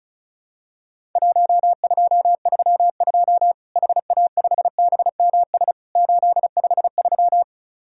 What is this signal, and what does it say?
Morse code, heard as a keyed tone, about 36 wpm, 700 Hz. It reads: 1232 HA5BMS 853
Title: Worked All Europe DX Contest - QTC Exchange - 35wpm